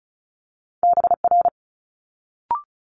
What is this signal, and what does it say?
Morse code, heard as a keyed tone, about 36 wpm, 700 Hz. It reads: BR
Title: Multi-letter Phonemes Quick Fire 35wpm